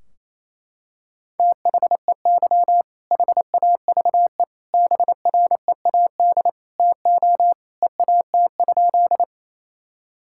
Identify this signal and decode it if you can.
Morse code, heard as a keyed tone, about 28 wpm, 700 Hz. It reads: THEY HAVE BREAD TO EAT?
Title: Sentences from Top 1600 Words 28wpm